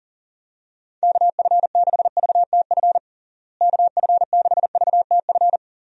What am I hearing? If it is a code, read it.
Morse code, heard as a keyed tone, about 40 wpm, 700 Hz. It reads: KF6VTF KF6VTF
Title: USA Call Signs - 6 Characters - Repeated 3x - 40wpm